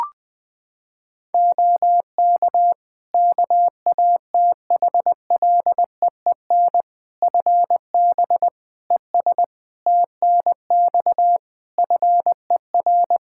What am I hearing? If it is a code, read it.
Morse code, heard as a keyed tone, about 20 wpm, 700 Hz. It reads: OK KATHLEEN FB ES TNX FER
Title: Practice QSO - Exchange 2 - Info - 20wpm